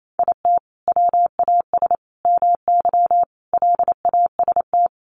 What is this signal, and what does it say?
Morse code, heard as a keyed tone, about 28 wpm, 700 Hz. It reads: IT WAS MY LAST
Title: Sentences from Top 1200 Words 28wpm